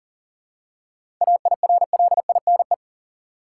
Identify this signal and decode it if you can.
Morse code, heard as a keyed tone, about 40 wpm, 700 Hz. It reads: AIRLINE
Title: Seven Letter Words - 40wpm